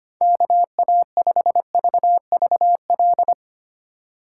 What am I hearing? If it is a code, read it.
Morse code, heard as a keyed tone, about 25 wpm, 700 Hz. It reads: KA5VVL
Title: Speed Racing - USA Call Signs 25wpm